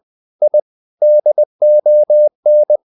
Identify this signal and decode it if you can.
Morse code, heard as a keyed tone, about 20 wpm, 600 Hz. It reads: I DON
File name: the old man and the sea / the old man and the sea part 1 20wpm